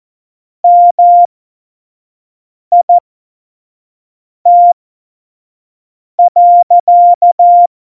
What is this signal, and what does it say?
Morse code, heard as a keyed tone, about 14 wpm, spaced as for 5 wpm, 700 Hz. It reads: MIT.
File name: Morsekurs / Witze / Witz 03. November 2020, 25 F70